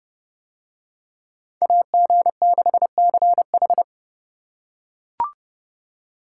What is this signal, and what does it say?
Morse code, heard as a keyed tone, about 29 wpm, 700 Hz. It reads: AG6CH